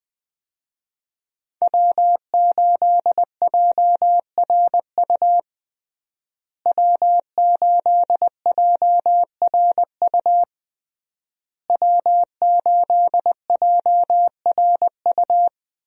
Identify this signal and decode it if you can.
Morse code, heard as a keyed tone, about 20 wpm, 700 Hz. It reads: W8JRU W8JRU W8JRU
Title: USA Call Signs - 5 Characters - Repeated 3x - 20wpm